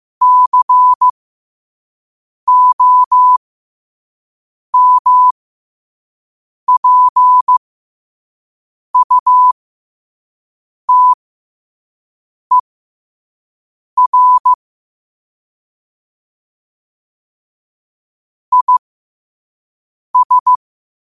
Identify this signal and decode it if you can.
Morse code, heard as a keyed tone, about 15 wpm, spaced as for 5 wpm, 1000 Hz. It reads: COMPUTER IS
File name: qotdfiles / 5wpm